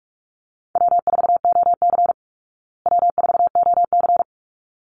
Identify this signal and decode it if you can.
Morse code, heard as a keyed tone, about 44 wpm, 700 Hz. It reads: W4OC W4OC
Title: Easy Call Signs Repeated 2x - 45wpm